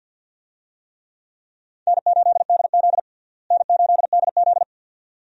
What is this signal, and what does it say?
Morse code, heard as a keyed tone, about 52 wpm, 700 Hz. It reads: N8DZ N8DZ
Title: USA Call Signs - 4 Characters - Repeated 2x - 50wpm